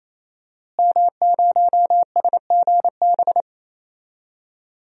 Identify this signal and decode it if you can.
Morse code, heard as a keyed tone, about 28 wpm, 700 Hz. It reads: M0SGB
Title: DXCC-100 Call Signs - 5 Characters - Repeated 3x - 28wpm